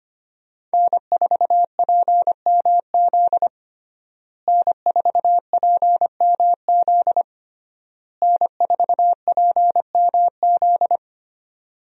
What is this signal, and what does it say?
Morse code, heard as a keyed tone, about 25 wpm, 700 Hz. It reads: N4PMZ N4PMZ N4PMZ
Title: USA Call Signs - 5 Characters - Repeated 3x - 25wpm